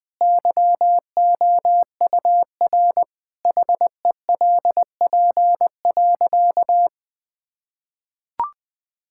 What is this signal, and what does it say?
Morse code, heard as a keyed tone, about 20 wpm, 700 Hz. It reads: YOUR HELP.
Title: Sentences with 5-Word Phrases - Spoken Beginning - 20wpm